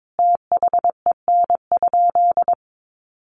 Morse code, heard as a keyed tone, about 22 wpm, 700 Hz. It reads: THEN?
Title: Sentences from Top 100 Words 22wpm